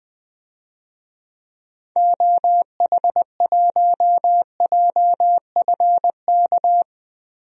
Morse code, heard as a keyed tone, about 20 wpm, 700 Hz. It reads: OH1JFK
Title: DXCC-100 Call Signs - 6 Characters - 20wpm